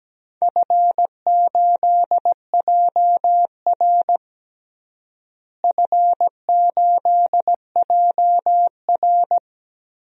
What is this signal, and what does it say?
Morse code, heard as a keyed tone, about 17 wpm, 700 Hz. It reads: F8JR F8JR
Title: DXCC-100 Call Signs - 4 Characters - Repeated 3x - 17wpm